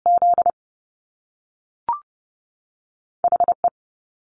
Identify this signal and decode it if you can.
Morse code, heard as a keyed tone, about 30 wpm, 700 Hz. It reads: Z HE
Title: Top Two Letter Combinations in Words 30wpm